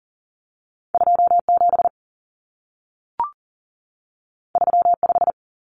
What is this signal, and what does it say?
Morse code, heard as a keyed tone, about 40 wpm, 700 Hz. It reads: 27 35
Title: Common Cut Numbers - 2-Digit 40wpm